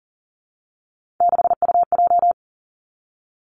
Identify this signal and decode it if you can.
Morse code, heard as a keyed tone, about 42 wpm, 700 Hz. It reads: 6UJ
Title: Three Challenging Letter-Number Combinations 40wpm